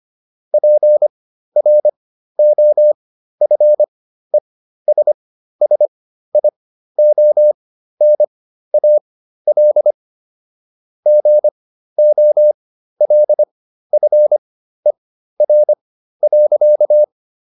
Morse code, heard as a keyed tone, about 25 wpm, spaced as for 13 wpm, 600 Hz. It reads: PROFESSIONAL GOLFER.